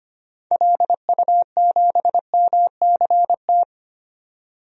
Morse code, heard as a keyed tone, about 25 wpm, 690 Hz. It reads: LU7MCT
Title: DXCC-100 Call Signs - 6 Characters - 25wpm